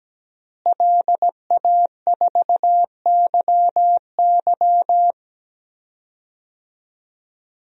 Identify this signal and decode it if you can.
Morse code, heard as a keyed tone, about 17 wpm, 700 Hz. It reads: LA4YY
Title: DXCC-100 Call Signs - 5 Characters - Repeated 2x - 17wpm